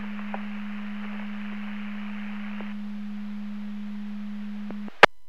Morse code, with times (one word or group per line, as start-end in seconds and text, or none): none